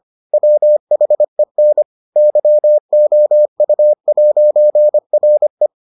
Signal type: Morse code

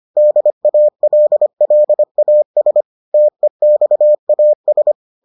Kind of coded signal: Morse code

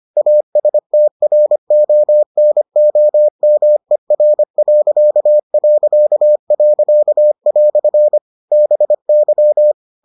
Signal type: Morse code

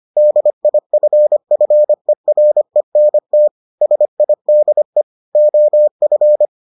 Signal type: Morse code

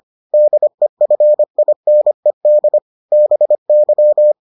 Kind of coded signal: Morse code

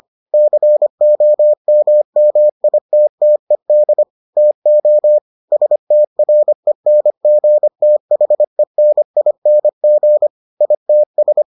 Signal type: Morse code

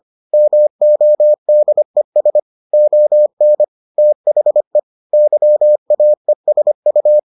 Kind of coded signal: Morse code

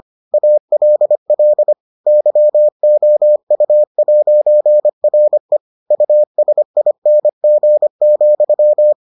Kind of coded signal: Morse code

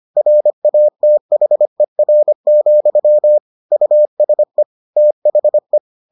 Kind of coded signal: Morse code

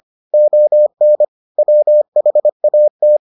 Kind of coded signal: Morse code

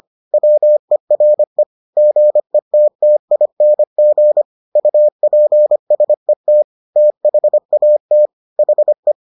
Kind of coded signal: Morse code

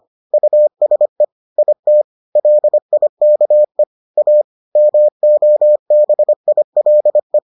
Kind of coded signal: Morse code

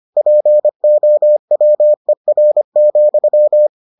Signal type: Morse code

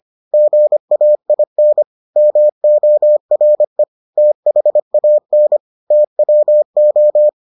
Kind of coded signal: Morse code

